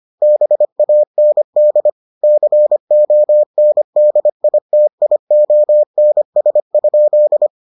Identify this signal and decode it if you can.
Morse code, heard as a keyed tone, about 25 wpm, 600 Hz. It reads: BAND CONDITIONS?